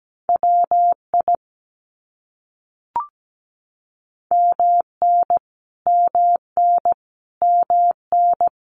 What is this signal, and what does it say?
Morse code, heard as a keyed tone, about 17 wpm, 700 Hz. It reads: WI MN MN MN
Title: US States Repeated 3x 17wpm